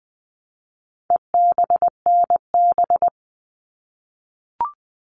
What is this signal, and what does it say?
Morse code, heard as a keyed tone, about 20 wpm, 700 Hz. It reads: EBNB
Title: All Cut Numbers - 4-Digit 20wpm